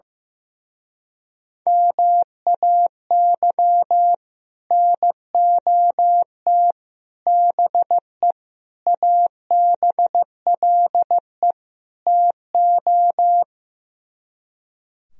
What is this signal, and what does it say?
Morse code, heard as a keyed tone, about 15 wpm, 700 Hz. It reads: MAY NOT BE ABLE TO
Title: Speed-Racing - Sets of 5 Words 15wpm